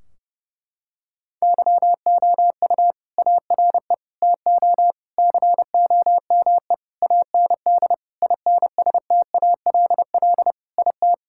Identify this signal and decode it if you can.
Morse code, heard as a keyed tone, about 30 wpm, 700 Hz. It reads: YOU ARE TO COME AND INSTALL IT
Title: Sentences from Top 1600 Words 30wpm